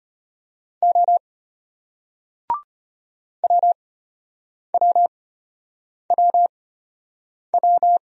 Morse code, heard as a keyed tone, about 33 wpm, spaced as for 8 wpm, 700 Hz. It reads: O WWWW